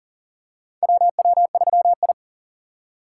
Morse code, heard as a keyed tone, about 37 wpm, 700 Hz. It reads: WW3I
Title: USA Call Signs - 4 Characters - 3x Character Spacing - 40wpm